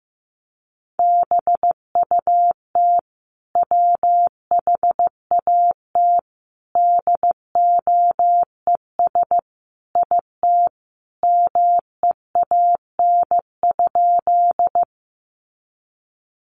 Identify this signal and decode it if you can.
Morse code, heard as a keyed tone, about 15 wpm, 700 Hz. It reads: BUT WHAT DOES IT MEAN?
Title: Sentences from Top 200 Words 15wpm